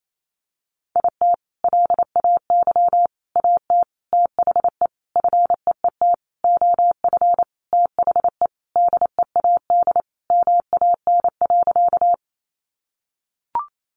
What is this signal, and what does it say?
Morse code, heard as a keyed tone, about 28 wpm, 700 Hz. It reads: IT LAY AT THE FEET OF THE DEAD MAN.
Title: Sentences from Top 1500 Words 28wpm